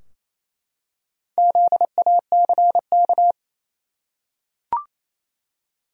Morse code, heard as a keyed tone, about 28 wpm, 700 Hz. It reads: ZACK
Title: USA Common Names - 28wpm